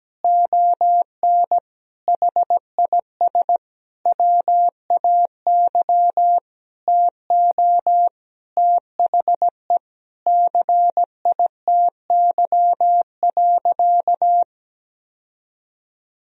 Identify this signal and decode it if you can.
Morse code, heard as a keyed tone, about 17 wpm, 700 Hz. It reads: ON HIS WAY TO THE CITY.
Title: Sentences from Top 1000 Words - Review Part 1 - 17wpm